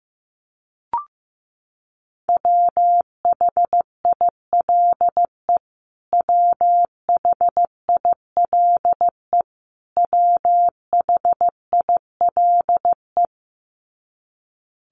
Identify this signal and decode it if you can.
Morse code, heard as a keyed tone, about 15 wpm, 700 Hz. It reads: WHILE WHILE WHILE
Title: Top 500 Words Repeated 15wpm